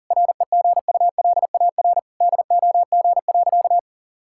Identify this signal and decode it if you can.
Morse code, heard as a keyed tone, about 40 wpm, 700 Hz. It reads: REGULAR DOG.
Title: Sentences from Top 1000 Words - Review Part 6 - 40wpm